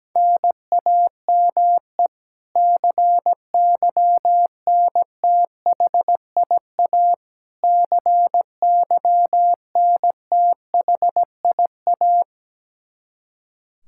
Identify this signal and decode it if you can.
Morse code, heard as a keyed tone, about 17 wpm, 700 Hz. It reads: NAME CYNTHIA CYNTHIA
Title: K1USN SST QSO Practice 17wpm